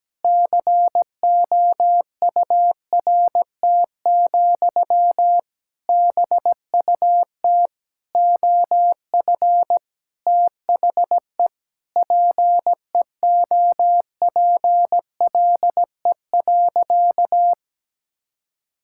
Morse code, heard as a keyed tone, about 17 wpm, 700 Hz. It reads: COURT, BUT OF THE PEOPLE.
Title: Sentences from Top 1600 Words 17wpm